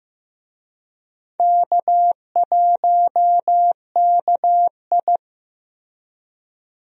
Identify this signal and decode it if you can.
Morse code, heard as a keyed tone, about 15 wpm, 700 Hz. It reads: K1KI
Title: USA Call Signs - 4 Characters - Repeated 2x - 15wpm